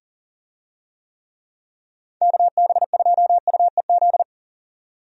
Morse code, heard as a keyed tone, about 38 wpm, 690 Hz. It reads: KB2UEZ
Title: USA Call Signs - 6 Characters - 40wpm